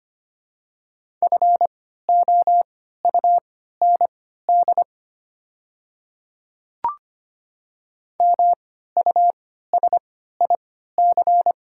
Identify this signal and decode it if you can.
Morse code, heard as a keyed tone, about 25 wpm, spaced as for 14 wpm, 700 Hz. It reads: FOUND MUSIC
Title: 3x Character Spacing - Top 300 Words - 25wpm